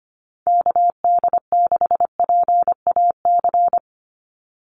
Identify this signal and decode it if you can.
Morse code, heard as a keyed tone, about 25 wpm, 700 Hz. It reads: KD6PAC
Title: USA Call Signs 25wpm